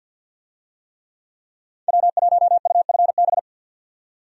Morse code, heard as a keyed tone, about 50 wpm, 700 Hz. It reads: W1UFB
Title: USA Call Signs - 5 Characters - 50wpm